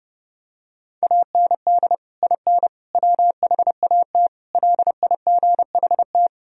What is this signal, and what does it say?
Morse code, heard as a keyed tone, about 30 wpm, 700 Hz. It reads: AND IN WHAT LIGHT
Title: Sentences from Top 1000 Words - Review Part 2 - 30wpm